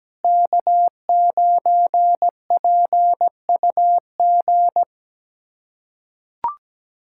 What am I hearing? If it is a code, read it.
Morse code, heard as a keyed tone, about 17 wpm, 700 Hz. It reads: K9PUG
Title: USA Callsign DE Callsign 17wpm